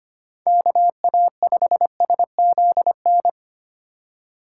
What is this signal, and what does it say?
Morse code, heard as a keyed tone, about 32 wpm, 700 Hz. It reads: KA5SZN